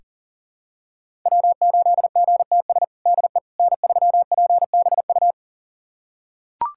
Morse code, heard as a keyed tone, about 40 wpm, 700 Hz. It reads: W8GTS DE N3PBU
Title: USA Callsign DE Callsign - 5 Characters - 40wpm